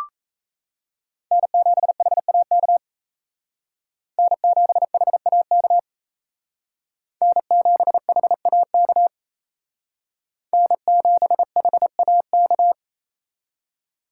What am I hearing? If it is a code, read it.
Morse code, heard as a keyed tone, about 34 wpm, 700 Hz. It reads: N7HAK N7HAK N7HAK N7HAK
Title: Speed Racing - USA 1x3 Call Signs 28wpm